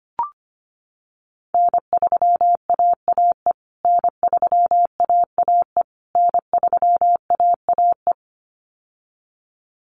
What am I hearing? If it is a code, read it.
Morse code, heard as a keyed tone, about 25 wpm, 700 Hz. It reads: N3AAE N3AAE N3AAE
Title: USA Call Signs Repeated 3x 25wpm